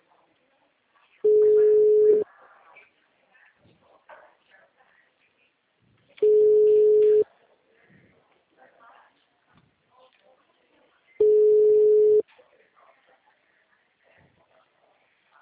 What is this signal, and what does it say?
Ringback tone (Europe, 425 Hz), with speech.